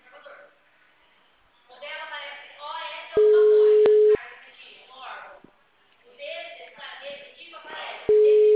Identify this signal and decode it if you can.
Ringback tone (Europe, 425 Hz), with speech.
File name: out- -2009- - - .28